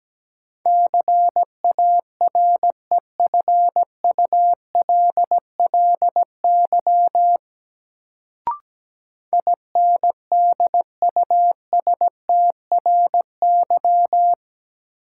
Morse code, heard as a keyed tone, about 17 wpm, 700 Hz. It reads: CAREFULLY INDUSTRY
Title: Sentences from Top 1300 Words - New Word Drill Alt - 17wpm